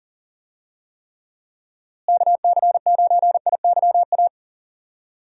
Morse code, heard as a keyed tone, about 40 wpm, 700 Hz. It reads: KC9IYA
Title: USA Call Signs - 6 Characters - Repeated 3x - 40wpm